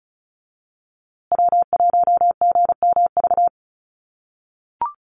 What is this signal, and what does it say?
Morse code, heard as a keyed tone, about 34 wpm, 700 Hz. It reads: W1GMV
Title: USA Call Signs Repeated 3x 35wpm